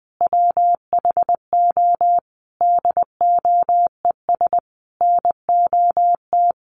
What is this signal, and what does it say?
Morse code, heard as a keyed tone, about 20 wpm, 700 Hz. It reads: WHO DOES NOT
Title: Sentences from Top 1500 Words 20wpm